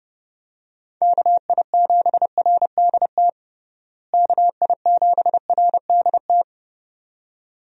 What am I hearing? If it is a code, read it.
Morse code, heard as a keyed tone, about 30 wpm, 700 Hz. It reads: KI7RDT KI7RDT
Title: USA Call Signs - 6 Characters - Repeated 2x - 30wpm